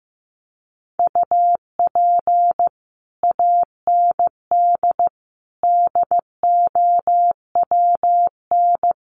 Morse code, heard as a keyed tone, about 15 wpm, 700 Hz. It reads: UP AND DOWN
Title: Sets of 4 Words 15wpm